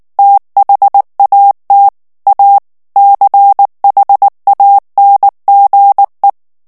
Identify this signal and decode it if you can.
Morse code, heard as a keyed tone, about 19 wpm, 800 Hz. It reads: THAT A CHANGE
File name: ANGLAIS texte clair 3 vitesse 18 mots minute